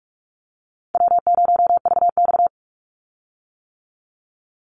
Morse code, heard as a keyed tone, about 44 wpm, 700 Hz. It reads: W0VX
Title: Easy Call Signs - 45wpm